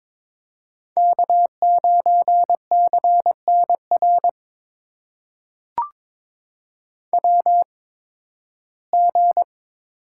Morse code, heard as a keyed tone, about 22 wpm, spaced as for 15 wpm, 700 Hz. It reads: K9CNR WG
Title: USA Call Signs - 8x Character Spacing - 22wpm